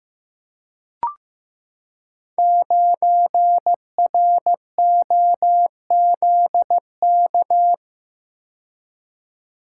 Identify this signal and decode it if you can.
Morse code, heard as a keyed tone, about 15 wpm, 700 Hz. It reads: 9ROZK
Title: Code Groups - 5 Letter:Numbers - 15wpm